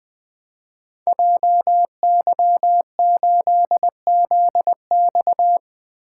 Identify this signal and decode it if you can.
Morse code, heard as a keyed tone, about 21 wpm, 700 Hz. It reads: JY8ZX
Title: Code Groups - 5 Letter:Numbers - 20wpm